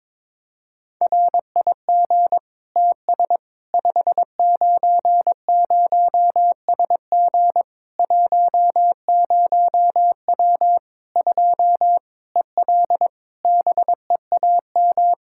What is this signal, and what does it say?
Morse code, heard as a keyed tone, about 21 wpm, 700 Hz. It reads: RIG TS 590SG 10W 2 EL BEAM
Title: Practice QSO - Exchange 2 - Info - 22wpm